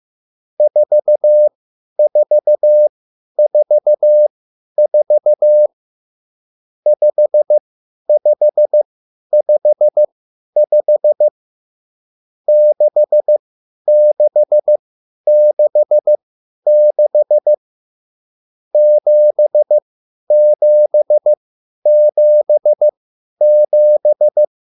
Morse code, heard as a keyed tone, about 15 wpm, spaced as for 10 wpm, 600 Hz. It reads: 4444 5555 6666 7777